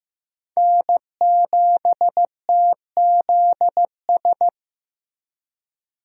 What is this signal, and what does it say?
Morse code, heard as a keyed tone, about 15 wpm, 700 Hz. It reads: N7TZS
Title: USA Call Signs - 5 Characters - 15wpm